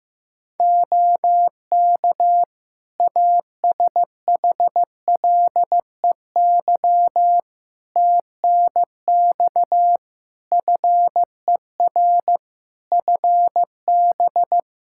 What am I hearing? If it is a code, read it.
Morse code, heard as a keyed tone, about 16 wpm, 700 Hz. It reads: OK ASHLEY TNX FER FB
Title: Practice QSO - Exchange 4 - Ending - 15wpm